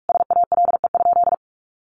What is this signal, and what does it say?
Morse code, heard as a keyed tone, about 44 wpm, 700 Hz. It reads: SALE?
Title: Sentences from Top 1400 Words 45wpm